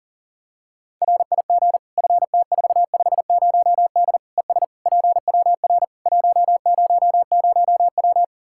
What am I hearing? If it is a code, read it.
Morse code, heard as a keyed tone, about 40 wpm, 700 Hz. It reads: RIG FT450D ES PWR 100W
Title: Practice QSO - Exchange 2 - Info - 40wpm